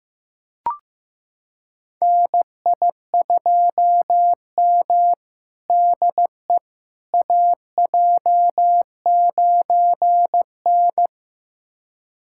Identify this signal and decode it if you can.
Morse code, heard as a keyed tone, about 15 wpm, 700 Hz. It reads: NI2M DE AJ9N
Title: USA Callsign DE Callsign - 4 Characters - 15wpm